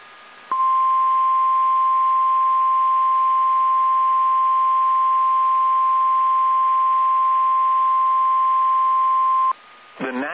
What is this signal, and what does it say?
Emergency Alert System (SAME) tones (the 1050 Hz attention tone). It is heard from 0.5 to 9.6 s.
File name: eas / KSDZKDJL / KSDZKDJL